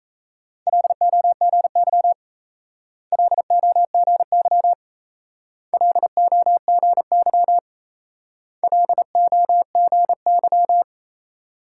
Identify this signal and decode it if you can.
Morse code, heard as a keyed tone, about 36 wpm, 700 Hz. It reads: LOGY LOGY LOGY LOGY